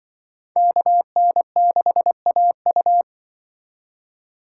Morse code, heard as a keyed tone, about 23 wpm, 700 Hz. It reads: KN6AU
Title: Speed Racing - USA 2x2 Call Signs 20wpm